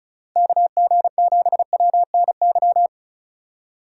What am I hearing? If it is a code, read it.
Morse code, heard as a keyed tone, about 36 wpm, 700 Hz. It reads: KG7WNY
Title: USA Call Signs - 6x Character Spacing - 35wpm